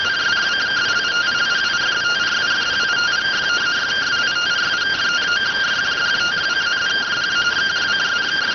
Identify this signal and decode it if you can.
Two-tone teleprinter signal, about 170 Hz shift.